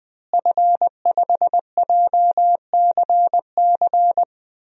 Morse code, heard as a keyed tone, about 20 wpm, 700 Hz. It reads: F5JCC
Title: DXCC-100 Call Signs - 5 Characters - Repeated 3x - 20wpm